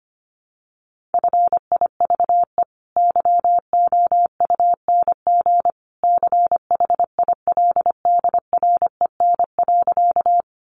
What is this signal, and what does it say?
Morse code, heard as a keyed tone, about 25 wpm, 700 Hz. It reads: FIVE YOUNG CHILDREN.
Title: Sentences from Top 400 Words 25wpm